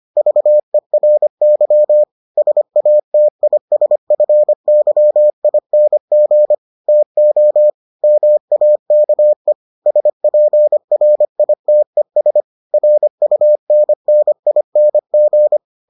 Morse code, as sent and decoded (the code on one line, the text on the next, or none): ...- . .-. -.-- / ... .- - .. ... ..-. -.-- .. -. --. / - --- / -- .- -.- . / ... .--. .-. .. - . ... / .-. ..- -. -. .. -. --.
VERY SATISFYING TO MAKE SPRITES RUNNING